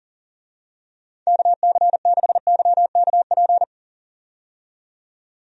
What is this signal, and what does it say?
Morse code, heard as a keyed tone, about 39 wpm, 700 Hz. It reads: KC6YKP
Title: USA Call Signs - 6 Characters - 40wpm